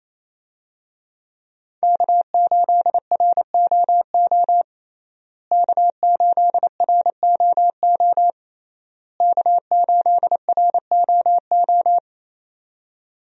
Morse code, heard as a keyed tone, about 28 wpm, 700 Hz. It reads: K8ROO K8ROO K8ROO
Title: USA Call Signs - 5 Characters - Repeated 3x - 28wpm